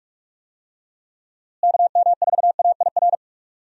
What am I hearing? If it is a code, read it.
Morse code, heard as a keyed tone, about 46 wpm, 700 Hz. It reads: KM4AIR
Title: USA Call Signs - 6 Characters - Repeated 2x - 45wpm